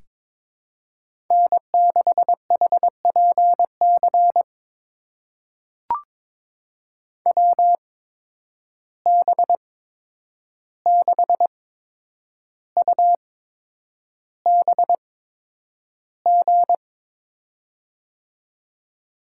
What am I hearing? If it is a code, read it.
Morse code, heard as a keyed tone, about 22 wpm, spaced as for 9 wpm, 700 Hz. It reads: N6HPC WB6UBG
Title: USA Call Signs - 8x Character Spacing - 22wpm